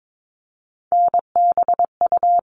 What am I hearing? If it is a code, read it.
Morse code, heard as a keyed tone, about 22 wpm, 700 Hz. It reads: NBU
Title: All Cut Numbers - 3-Digit 22wpm